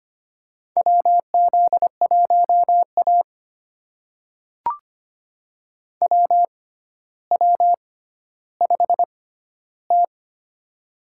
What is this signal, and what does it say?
Morse code, heard as a keyed tone, about 25 wpm, spaced as for 13 wpm, 700 Hz. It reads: WZ1A WW5T
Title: USA Call Signs - 6x Character Spacing - 25wpm